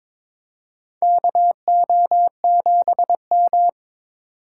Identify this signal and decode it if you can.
Morse code, heard as a keyed tone, about 22 wpm, 700 Hz. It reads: KO7M
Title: USA Call Signs - 4 Characters - 4x Character Spacing - 22wpm